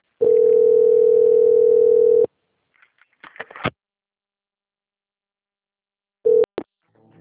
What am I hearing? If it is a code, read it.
Ringback tone (North America), 440 + 480 Hz.